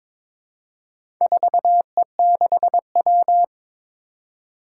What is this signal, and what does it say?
Morse code, heard as a keyed tone, about 22 wpm, 700 Hz. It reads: 4E6W
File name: practice / 22wpm / Four Letter-Number Combinations - 22wpm